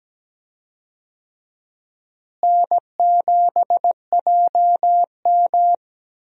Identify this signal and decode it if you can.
Morse code, heard as a keyed tone, about 17 wpm, 700 Hz. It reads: N7JM
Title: USA Call Signs - 4 Characters - Repeated 3x - 17wpm